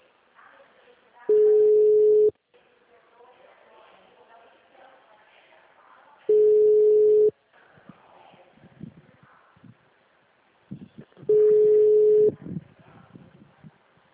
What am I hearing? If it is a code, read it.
Ringback tone (Europe, 425 Hz), with speech.